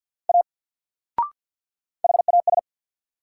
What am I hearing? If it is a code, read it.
Morse code, heard as a keyed tone, about 48 wpm, 700 Hz. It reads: A HAS
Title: Sentences from Top 100 Words - New Word Drill - 50wpm